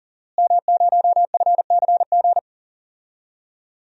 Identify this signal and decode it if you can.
Morse code, heard as a keyed tone, about 40 wpm, 700 Hz. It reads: M0FCG
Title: DXCC-100 Call Signs - 5 Characters - Repeated 2x - 40wpm